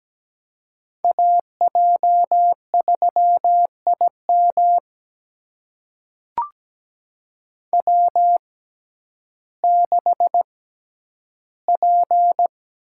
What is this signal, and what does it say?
Morse code, heard as a keyed tone, about 17 wpm, spaced as for 10 wpm, 700 Hz. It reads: AJ3IM W6P
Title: USA Call Signs - 5 Characters - 6x Character Spacing - 17wpm